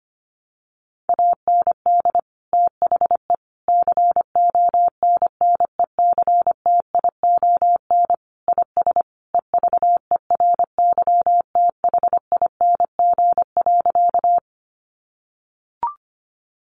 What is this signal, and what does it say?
Morse code, heard as a keyed tone, about 25 wpm, 700 Hz. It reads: AND THE CONNECTION IS EVERYTHING.
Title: Sentences from Top 1300 Words 25wpm